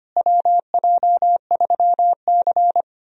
Morse code, heard as a keyed tone, about 25 wpm, 700 Hz. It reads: WJ3C
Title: USA Call Signs - 4 Characters - Repeated 2x - 25wpm